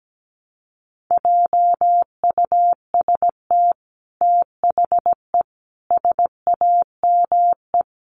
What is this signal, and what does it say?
Morse code, heard as a keyed tone, about 17 wpm, 700 Hz. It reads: JUST THE SAME
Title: Sentences from Top 900 Words 17wpm